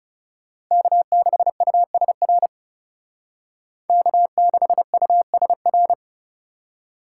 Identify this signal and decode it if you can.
Morse code, heard as a keyed tone, about 36 wpm, 700 Hz. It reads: K6USR K6USR
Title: Speed Racing - USA 1x3 Call Signs 30wpm